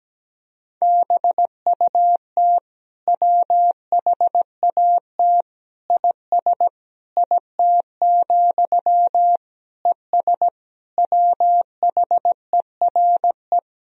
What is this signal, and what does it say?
Morse code, heard as a keyed tone, about 17 wpm, 700 Hz. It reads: BUT WHAT IS IT, ES WHERE
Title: Sentences from Top 300 Words and Abbreviations 17wpm